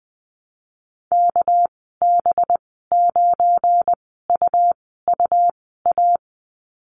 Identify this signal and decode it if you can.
Morse code, heard as a keyed tone, about 20 wpm, spaced as for 14 wpm, 700 Hz. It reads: KB9UUA
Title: 231 Lesson 37g - Letter x - Call Signs 20wpm